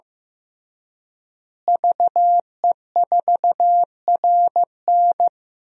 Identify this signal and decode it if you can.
Morse code, heard as a keyed tone, about 15 wpm, 700 Hz. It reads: VE4RN
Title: DXCC-100 Call Signs - 5 Characters - Repeated 3x - 15wpm